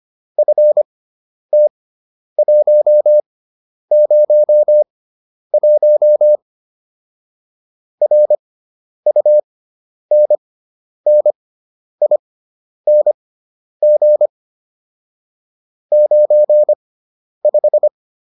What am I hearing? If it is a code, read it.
Morse code, heard as a keyed tone, about 25 wpm, spaced as for 10 wpm, 600 Hz. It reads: FT101 RUNNING 95
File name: QSO409 10